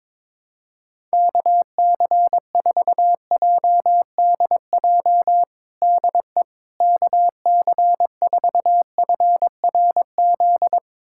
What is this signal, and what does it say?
Morse code, heard as a keyed tone, about 22 wpm, 700 Hz. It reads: KC4JDJ DE KC4FRZ
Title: USA Callsign DE Callsign - 6 Characters - 22wpm